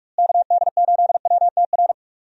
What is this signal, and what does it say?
Morse code, heard as a keyed tone, about 43 wpm, 700 Hz. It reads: KD8WTR
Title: USA Call Signs - 6 Characters - Repeated 3x - 45wpm